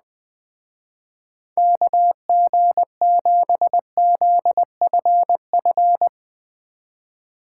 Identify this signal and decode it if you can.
Morse code, heard as a keyed tone, about 20 wpm, 700 Hz. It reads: KG7ZFF